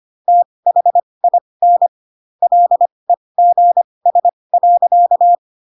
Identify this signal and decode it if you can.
Morse code, heard as a keyed tone, about 25 wpm, spaced as for 20 wpm, 700 Hz. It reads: THIN LEGS.
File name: Pinocchio / Pinocchio ch2 15 25 ik1hge